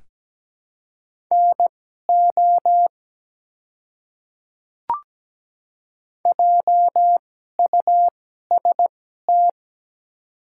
Morse code, heard as a keyed tone, about 17 wpm, spaced as for 12 wpm, 700 Hz. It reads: NO JUST
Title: 2x Character Spacing - Top 300 Words - 17wpm